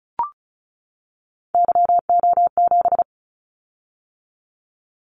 Morse code, heard as a keyed tone, about 36 wpm, 690 Hz. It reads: YO7
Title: Callsign Prefixes for Top 25 DXCC - 35wpm